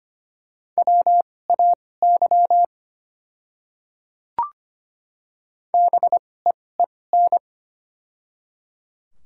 Morse code, heard as a keyed tone, about 25 wpm, spaced as for 18 wpm, 690 Hz. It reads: WAY BEEN
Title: 2x Character Spacing - Top 100 Words - 25wpm